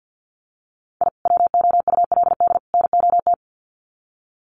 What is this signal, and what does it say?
Morse code, heard as a keyed tone, about 50 wpm, 700 Hz. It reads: I WOULD NOT
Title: Sets of 3 Words 50wpm